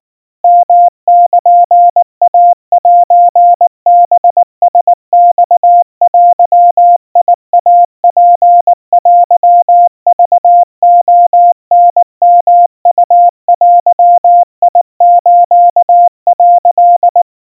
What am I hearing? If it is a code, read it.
Morse code (Japanese Wabun code), heard as a keyed tone, about 19 wpm, 700 Hz. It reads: ヨルイセハラマデイツテクレタヨウデス」
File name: kawa / Morse sample / kamyu 70 cpm 700 Hz Crispy